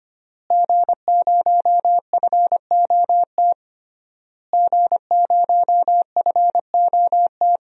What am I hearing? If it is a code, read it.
Morse code, heard as a keyed tone, about 25 wpm, 700 Hz. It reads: G0FOT G0FOT
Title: DXCC-100 Call Signs - 5 Characters - Repeated 3x - 25wpm